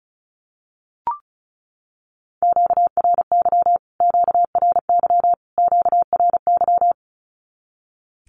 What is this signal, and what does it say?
Morse code, heard as a keyed tone, about 34 wpm, 700 Hz. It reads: QRY QRY QRY
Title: Q-Codes Repeated 3x 35wpm